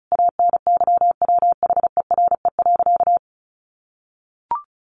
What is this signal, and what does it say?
Morse code, heard as a keyed tone, about 34 wpm, 700 Hz. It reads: ANYWHERE.
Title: Sentences from Top 1400 Words 35wpm